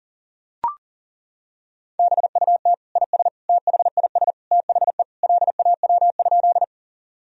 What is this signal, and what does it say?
Morse code, heard as a keyed tone, about 40 wpm, 700 Hz. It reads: BUT IS THIS THE LAW?
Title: Sentences from Top 1000 Words - Review Part 3 - 40wpm